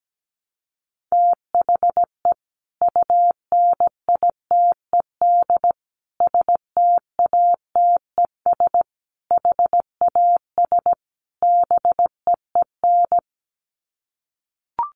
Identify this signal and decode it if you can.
Morse code, heard as a keyed tone, about 17 wpm, 700 Hz. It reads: THE UNITED STATES HAS BEEN
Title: Sets of 5 Words Encore2 17wpm